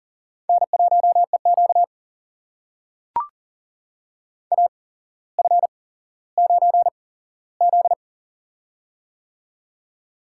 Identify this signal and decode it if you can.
Morse code, heard as a keyed tone, about 38 wpm, spaced as for 17 wpm, 700 Hz. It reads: N1EQ AF9Z
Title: USA Call Signs - 4 Characters - 8x Character Spacing - 40wpm